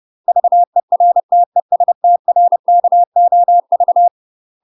Morse code, heard as a keyed tone, about 30 wpm, 700 Hz. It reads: VERT ES TRKOV
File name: UK MASTER QSOs TRAINING / 30WPM / UK MASTER QSOs 0040